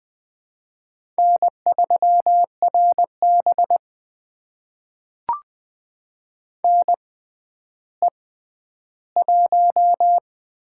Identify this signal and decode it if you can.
Morse code, heard as a keyed tone, about 20 wpm, spaced as for 11 wpm, 700 Hz. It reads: N3RB NE1
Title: USA Call Signs - 4 Characters - 6x Character Spacing - 20wpm